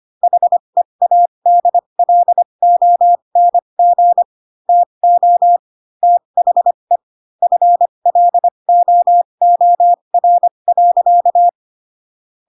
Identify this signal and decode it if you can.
Morse code, heard as a keyed tone, about 25 wpm, spaced as for 22 wpm, 700 Hz. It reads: HEADLONG TO THE FLOOR.
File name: Pinocchio / Pinocchio ch7 18 25 ik1hge